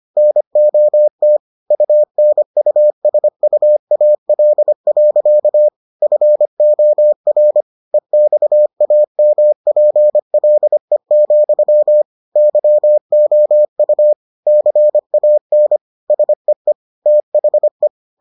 Morse code, heard as a keyed tone, about 25 wpm, 600 Hz. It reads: NOT UNUSUAL. FOR EXAMPLE, YOU CAN SEE THE